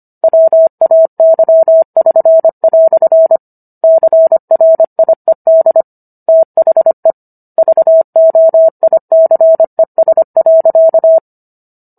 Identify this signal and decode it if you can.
Morse code, heard as a keyed tone, about 25 wpm, 650 Hz. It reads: WAY<SN>" CRIED THE VOICES.